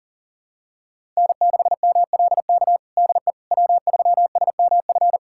Morse code, heard as a keyed tone, about 40 wpm, 700 Hz. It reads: N6MLK DE W3SMF
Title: USA Callsign DE Callsign - 5 Characters - 40wpm